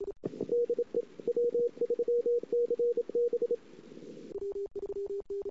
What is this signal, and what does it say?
Morse code, heard as a keyed tone, about 26 wpm, 480 Hz. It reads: DE W3CB
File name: CW / 2008-ss-cw-k5tr-op-n5rz -22:00